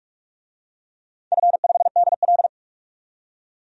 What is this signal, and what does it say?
Morse code, heard as a keyed tone, about 44 wpm, 700 Hz. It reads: F5DL
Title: DXCC-100 Call Signs - 4 Characters - 45wpm